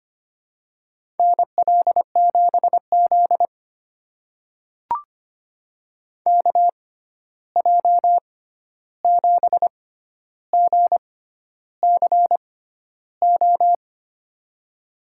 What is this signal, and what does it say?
Morse code, heard as a keyed tone, about 25 wpm, spaced as for 11 wpm, 700 Hz. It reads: NL7Z KJ7GCO
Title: USA Call Signs - 6x Character Spacing - 25wpm